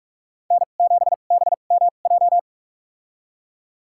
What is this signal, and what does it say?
Morse code, heard as a keyed tone, about 46 wpm, spaced as for 32 wpm, 700 Hz. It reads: N7BMJ
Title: USA Call Signs - 5 Characters - 2x Character Spacing - 45wpm